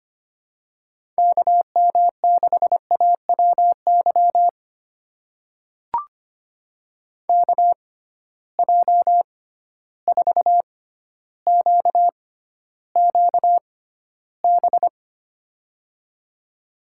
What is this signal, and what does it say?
Morse code, heard as a keyed tone, about 25 wpm, spaced as for 13 wpm, 700 Hz. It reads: KM6AWY KJ4QQB
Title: USA Call Signs - 6 Characters - 6x Character Spacing - 25wpm